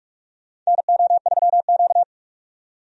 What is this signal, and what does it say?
Morse code, heard as a keyed tone, about 46 wpm, 700 Hz. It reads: NO3Q
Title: USA Call Signs - 4 Characters - 45wpm